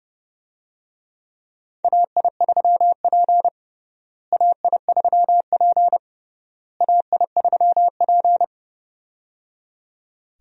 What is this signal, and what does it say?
Morse code, heard as a keyed tone, about 30 wpm, 700 Hz. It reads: AI3P AI3P AI3P
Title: USA Call Signs - 4 Characters - Repeated 3x - 30wpm